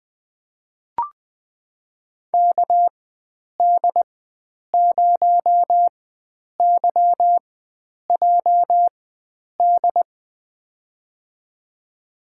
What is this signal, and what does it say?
Morse code, heard as a keyed tone, about 21 wpm, spaced as for 9 wpm, 700 Hz. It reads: KD0YJD
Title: USA Call Signs - 6 Characters - 4x Character Spacing - 20wpm